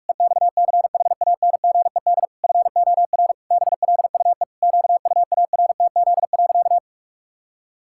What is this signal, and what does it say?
Morse code, heard as a keyed tone, about 41 wpm, 700 Hz. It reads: EXCHANGED FOR BLUE QUARTZ.
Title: Pangram Challenge v2 - 45wpm